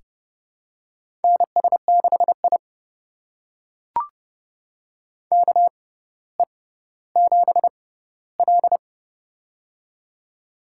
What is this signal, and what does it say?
Morse code, heard as a keyed tone, about 32 wpm, spaced as for 16 wpm, 700 Hz. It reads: NS6I KE7L
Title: USA Call Signs - 6x Character Spacing - 30wpm